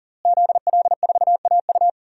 Morse code, heard as a keyed tone, about 40 wpm, 690 Hz. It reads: ZL4AU
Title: DXCC-100 Call Signs - 5 Characters - Repeated 2x - 40wpm